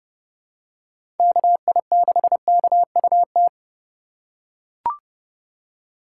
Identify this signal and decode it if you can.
Morse code, heard as a keyed tone, about 30 wpm, 700 Hz. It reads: KI6KUT
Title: USA Call Signs - 6 Characters - 2x Character Spacing - 30wpm